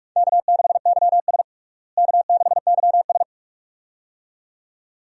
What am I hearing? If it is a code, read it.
Morse code, heard as a keyed tone, about 41 wpm, 700 Hz. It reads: K6YS K6YS
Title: USA Call Signs - 4 Characters - Repeated 2x - 45wpm